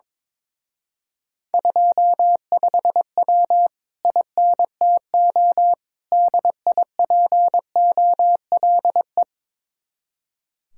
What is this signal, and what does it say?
Morse code, heard as a keyed tone, about 22 wpm, 700 Hz. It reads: 25W INTO DIPOLE
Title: Speed-Racing Common QSO Phrases - 22wpm